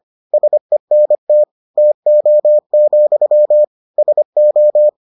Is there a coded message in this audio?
Yes: Morse code from 0.3 to 4.9 s, heard as a keyed tone, about 25 wpm, 600 Hz. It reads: SENT TO, SO